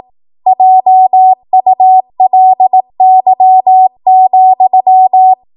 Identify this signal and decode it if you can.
Morse code, heard as a keyed tone, about 18 wpm, 750 Hz. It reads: JULY,
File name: Morse / 18WPM